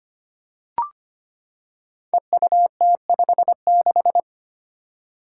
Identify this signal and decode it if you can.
Morse code, heard as a keyed tone, about 25 wpm, 700 Hz. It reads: EUT56
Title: Code Groups - 5 Letter:Numbers - 25wpm